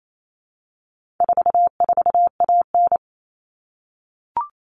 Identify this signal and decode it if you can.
Morse code, heard as a keyed tone, about 28 wpm, 700 Hz. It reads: 44AN
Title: All Cut Numbers - 4-Digit 28wpm